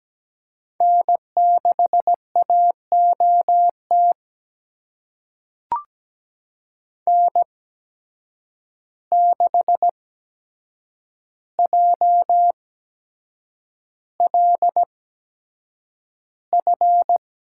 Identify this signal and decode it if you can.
Morse code, heard as a keyed tone, about 17 wpm, spaced as for 7 wpm, 700 Hz. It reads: N6AOT N6JLF
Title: USA Call Signs - 5 Characters - 8x Character Spacing - 17wpm